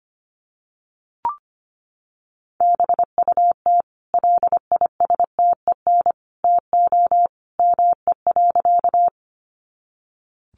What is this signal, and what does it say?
Morse code, heard as a keyed tone, about 25 wpm, 700 Hz. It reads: BUT LISTEN TO ME.
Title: Sentences from Top 400 Words 25wpm